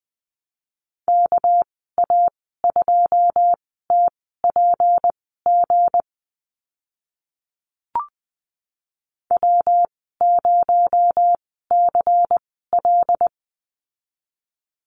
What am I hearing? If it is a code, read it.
Morse code, heard as a keyed tone, about 20 wpm, spaced as for 14 wpm, 700 Hz. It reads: KA2TPG W0CL
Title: 238 Lesson 38g - Letter q - Call Signs 20wpm